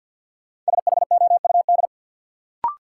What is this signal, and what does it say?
Morse code, heard as a keyed tone, about 48 wpm, 700 Hz. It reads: SHOUD
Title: Sentences from Top 1000 Words - Review Part 4 - New Word Drill - 50wpm